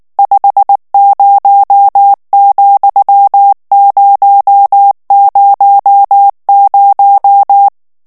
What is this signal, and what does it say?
Morse code, heard as a keyed tone, about 19 wpm, 800 Hz. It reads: 50,000
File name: morse mp3 / ANGLAIS texte clair 2 vitesse 18 mots minute